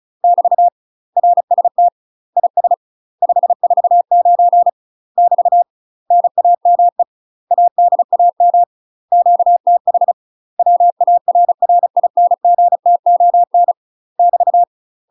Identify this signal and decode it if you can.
Morse code, heard as a keyed tone, about 36 wpm, 700 Hz. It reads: <BT> RST IS 549 <BT> NAME ADAM QTH WARRINGTON <BT>
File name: UK MASTER QSOs TRAINING / 35WPM / UK MASTER QSOs 0014